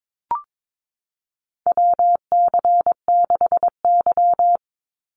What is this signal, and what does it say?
Morse code, heard as a keyed tone, about 22 wpm, 700 Hz. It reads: WC6Y
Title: Call Signs Repeated 2x - 22wpm